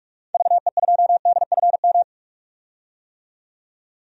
Morse code, heard as a keyed tone, about 46 wpm, 700 Hz. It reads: VE2DFM
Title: DXCC-100 Call Signs - 6 Characters - 45wpm